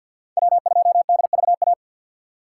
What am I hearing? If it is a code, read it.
Morse code, heard as a keyed tone, about 47 wpm, 700 Hz. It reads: W2DVA
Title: USA Call Signs - 5 Characters - 3x Character Spacing - 50wpm